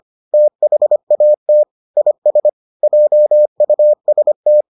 Morse code, heard as a keyed tone, about 25 wpm, 600 Hz. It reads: THAT IS JUST